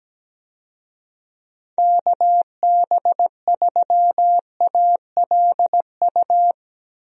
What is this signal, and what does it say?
Morse code, heard as a keyed tone, about 17 wpm, 700 Hz. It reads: KB3ALU
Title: USA Call Signs - 6 Characters - 17wpm